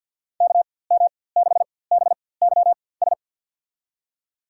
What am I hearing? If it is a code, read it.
Morse code, heard as a keyed tone, about 49 wpm, spaced as for 23 wpm, 700 Hz. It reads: KM6BYS
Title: USA Call Signs - 6 Characters - 4x Character Spacing - 50wpm